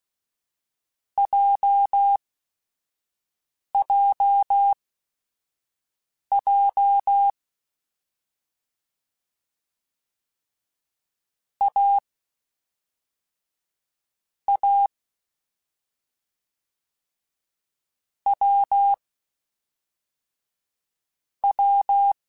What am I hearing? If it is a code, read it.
Morse code, heard as a keyed tone, about 16 wpm, spaced as for 4 wpm, 800 Hz. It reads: JJJ AA WW